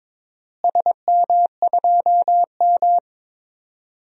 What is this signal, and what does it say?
Morse code, heard as a keyed tone, about 22 wpm, 700 Hz. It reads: SM2M
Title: DXCC-100 Call Signs - 4 Characters - Repeated 2x - 22wpm